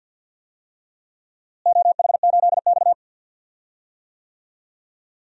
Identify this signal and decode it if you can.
Morse code, heard as a keyed tone, about 51 wpm, 700 Hz. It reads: OH8X
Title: DXCC-100 Call Signs - 4 Characters - Repeated 2x - 50wpm